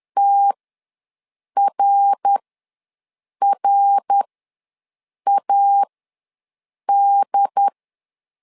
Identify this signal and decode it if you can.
Morse code, heard as a keyed tone, about 10 wpm, spaced as for 6 wpm, 800 Hz. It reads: TRRAD
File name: LEZIONI / 3 MWRO / 3 MWR ws 10wpm 2151